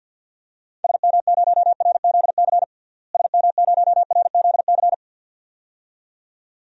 Morse code, heard as a keyed tone, about 48 wpm, 700 Hz. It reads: SM0RZC SM0RZC
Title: DXCC-100 Call Signs - 6 Characters - Repeated 2x - 50wpm